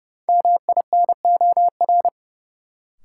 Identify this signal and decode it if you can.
Morse code, heard as a keyed tone, about 30 wpm, 700 Hz. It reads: MINOR
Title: Sentences from Top 1500 Words - New Word Drill Alt - 30wpm